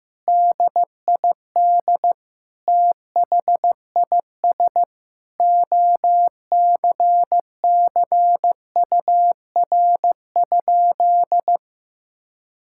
Morse code, heard as a keyed tone, about 16 wpm, 700 Hz. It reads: DID THIS OCCUR?
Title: Sentences from Top 1000 Words - Review Part 6 - 15wpm